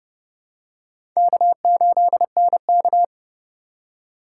Morse code, heard as a keyed tone, about 30 wpm, 700 Hz. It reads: K8NK
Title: USA Call Signs - 4 Characters - 3x Character Spacing - 30wpm